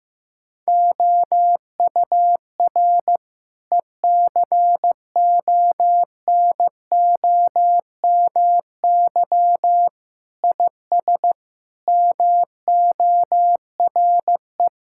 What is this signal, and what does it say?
Morse code, heard as a keyed tone, about 15 wpm, 700 Hz. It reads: OUR ECONOMY IS MORE
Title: Sentences from Top 1000 Words - Review Part 6 - 15wpm